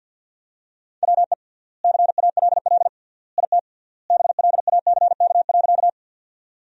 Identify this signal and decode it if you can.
Morse code, heard as a keyed tone, about 52 wpm, 700 Hz. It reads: WE CALL IT BLACK.
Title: 3x Word-Spacing - Sentences from Top 400 Words - 50wpm